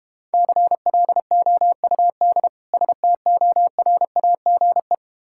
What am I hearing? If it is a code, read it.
Morse code, heard as a keyed tone, about 32 wpm, 700 Hz. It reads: CLOUD STORAGE